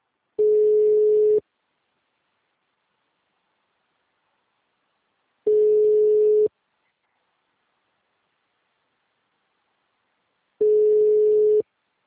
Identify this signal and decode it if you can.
Ringback tone (Europe, 425 Hz).